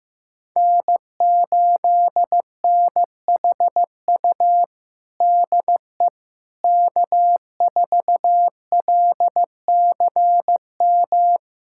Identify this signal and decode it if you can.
Morse code, heard as a keyed tone, about 15 wpm, 700 Hz. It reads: N8NHU DE K4LCM
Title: USA Callsign DE Callsign - 5 Characters - 15wpm